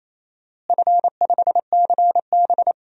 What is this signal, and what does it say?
Morse code, heard as a keyed tone, about 27 wpm, 700 Hz. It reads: F5CB
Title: DXCC-100 Call Signs - 4 Characters - Repeated 2x - 28wpm